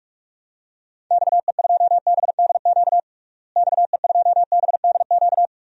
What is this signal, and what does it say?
Morse code, heard as a keyed tone, about 43 wpm, 700 Hz. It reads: XE2BDQ XE2BDQ
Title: DXCC-100 Call Signs - 6 Characters - Repeated 2x - 45wpm